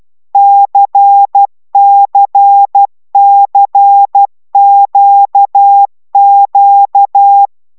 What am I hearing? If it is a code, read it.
Morse code, heard as a keyed tone, about 12 wpm, 800 Hz. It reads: CCCQQ
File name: taviro tanulas / cw 75 15 10